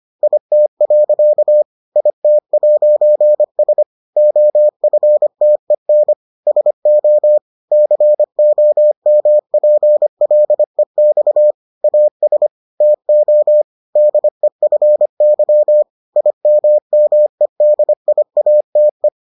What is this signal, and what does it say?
Morse code, heard as a keyed tone, about 25 wpm, 600 Hz. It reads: IT. IT'S OFTEN SO COMPLEX AS TO DEFY IMMEDIATE